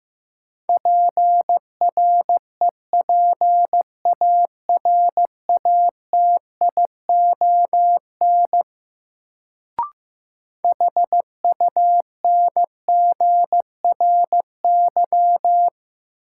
Morse code, heard as a keyed tone, about 15 wpm, 700 Hz. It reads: PREPARATION HUNGRY
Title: Sentences from Top 1400 Words - New Word Drill - 15wpm